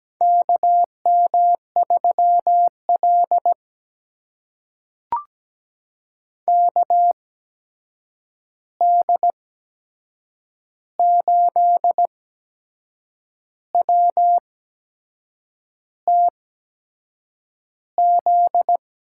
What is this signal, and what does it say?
Morse code, heard as a keyed tone, about 17 wpm, spaced as for 6 wpm, 700 Hz. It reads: KM3L KD8WTZ
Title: USA Call Signs - 8x Character Spacing - 17wpm